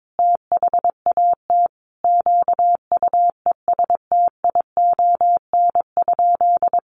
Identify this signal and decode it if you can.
Morse code, heard as a keyed tone, about 22 wpm, 700 Hz. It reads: THAT QUESTION?
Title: Sentences from Top 400 Words 22wpm